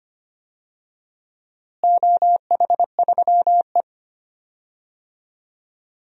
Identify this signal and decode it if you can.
Morse code, heard as a keyed tone, about 25 wpm, 700 Hz. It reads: OH3E
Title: DXCC-100 Call Signs - 4 Characters - Repeated 3x - 25wpm